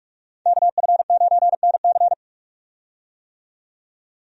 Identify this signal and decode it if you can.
Morse code, heard as a keyed tone, about 46 wpm, 700 Hz. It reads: KF9NC
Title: USA Call Signs - 5 Characters - 45wpm